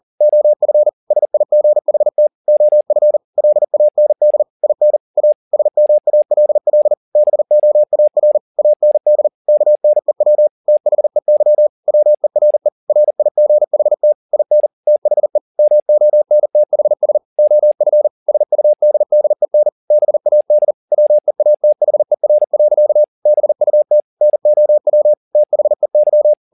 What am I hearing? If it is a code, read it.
Morse code, heard as a keyed tone, about 40 wpm, 600 Hz. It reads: OF SIGHT OF LAND IN A SMALL BOAR AND KNEW THEY WERE RIGHT IN THE MONTHS OF SUDDEN BAD WEATHER. BUT NOW THEY